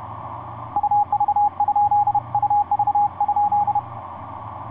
Morse code, heard as a keyed tone, about 31 wpm, 820 Hz. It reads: UV? UV?
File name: z21dxi sp4ndu etc 17m cw 11feb2010 1404